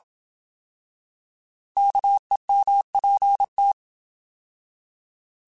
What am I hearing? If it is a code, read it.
Morse code, heard as a keyed tone, about 27 wpm, 800 Hz. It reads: KEMPT
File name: Alphabetical / Alphabetical mp3s / Ks @25